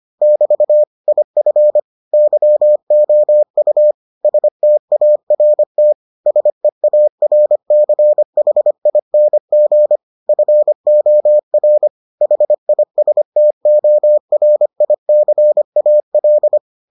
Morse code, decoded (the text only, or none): <BT> IF YOU START SEARCHING FOR HISTORICAL